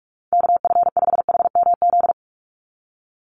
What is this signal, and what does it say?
Morse code, heard as a keyed tone, about 44 wpm, 700 Hz. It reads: KF5HMZ
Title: USA Call Signs 45wpm